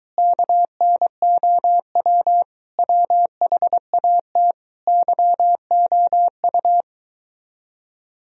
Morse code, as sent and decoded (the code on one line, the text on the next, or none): -.- -. --- .-- / .-- .... .- - / -.-- --- ..-
KNOW WHAT YOU